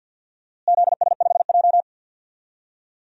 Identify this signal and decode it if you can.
Morse code, heard as a keyed tone, about 52 wpm, 700 Hz. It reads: ZS5J
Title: DXCC-100 Call Signs - 4 Characters - Repeated 3x - 50wpm